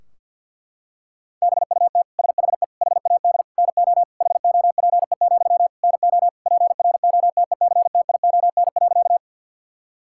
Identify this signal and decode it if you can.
Morse code, heard as a keyed tone, about 50 wpm, 700 Hz. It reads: BUT SHE HAD NO HOPE, NO PROTECTION.
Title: Sentences from Top 1600 Words 50wpm